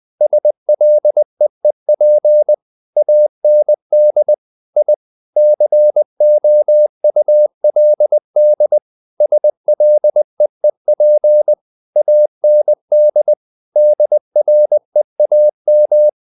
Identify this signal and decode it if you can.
Morse code, heard as a keyed tone, about 20 wpm, 600 Hz. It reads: SLEEP AND I COULD SLEEP AND DREAM
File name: the old man and the sea / the old man and the sea part 1 20wpm